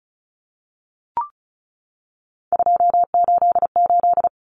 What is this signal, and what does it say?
Morse code, heard as a keyed tone, about 34 wpm, 700 Hz. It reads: 288
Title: Common Cut Numbers - 3-Digit 35wpm